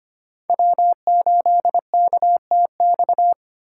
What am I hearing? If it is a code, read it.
Morse code, heard as a keyed tone, about 32 wpm, 700 Hz. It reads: W8KTX